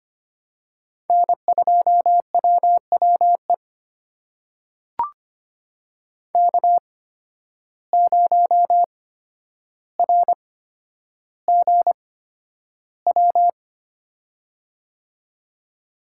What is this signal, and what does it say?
Morse code, heard as a keyed tone, about 25 wpm, spaced as for 11 wpm, 700 Hz. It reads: N2WWE K0RGW
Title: USA Call Signs - 5 Characters - 8x Character Spacing - 25wpm